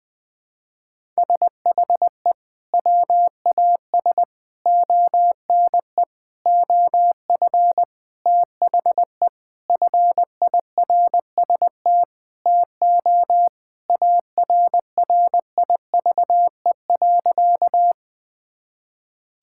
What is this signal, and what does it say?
Morse code, heard as a keyed tone, about 20 wpm, 700 Hz. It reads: SHE WAS ONE OF THE FIRST TO ARRIVE.
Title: Sentences with 6-Word Phrases - 20wpm